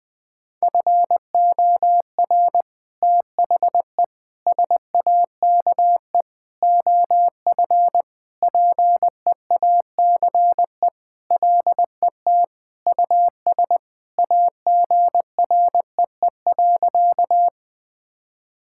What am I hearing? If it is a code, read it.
Morse code, heard as a keyed tone, about 20 wpm, 700 Hz. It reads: FOR THE SAKE OF PEACE LET US AGREE.
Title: Sentences with 4-Word Phrases - 20wpm